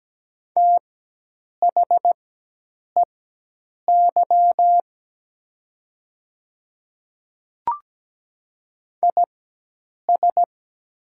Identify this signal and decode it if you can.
Morse code, heard as a keyed tone, about 17 wpm, spaced as for 8 wpm, 700 Hz. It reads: THEY IS
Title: 4x Character Spacing - Top 100 Words - 17wpm